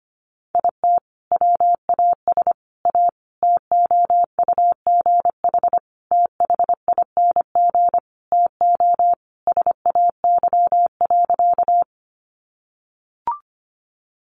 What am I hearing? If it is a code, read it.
Morse code, heard as a keyed tone, about 25 wpm, 700 Hz. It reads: IT WAS A TOUGH THING TO SAY.
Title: Sentences from Top 1200 Words 25wpm